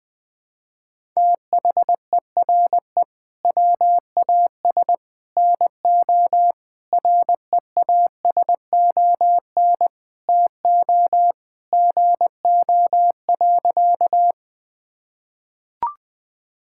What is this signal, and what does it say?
Morse code, heard as a keyed tone, about 21 wpm, 700 Hz. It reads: THERE WAS NO REASON TO GO.
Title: Sentences with 3-Word Phrases - 20wpm